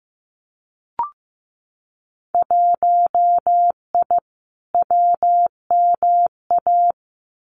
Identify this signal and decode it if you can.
Morse code, heard as a keyed tone, about 15 wpm, 700 Hz. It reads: 1I WMA
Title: Winter Field Day Exchanges Encore 15wpm